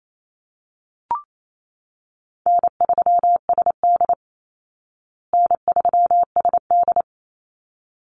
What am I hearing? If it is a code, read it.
Morse code, heard as a keyed tone, about 28 wpm, 700 Hz. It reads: N3SD N3SD
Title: Easy Call Signs Repeated 2x - 28wpm